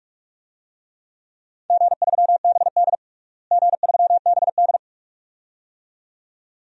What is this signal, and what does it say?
Morse code, heard as a keyed tone, about 46 wpm, 700 Hz. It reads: G3BD G3BD
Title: DXCC-100 Call Signs - 4 Characters - Repeated 2x - 45wpm